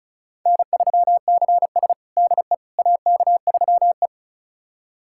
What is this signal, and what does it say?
Morse code, heard as a keyed tone, about 36 wpm, 700 Hz. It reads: N3CS DE AK3E
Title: USA Callsign DE Callsign - 4 Characters - 35wpm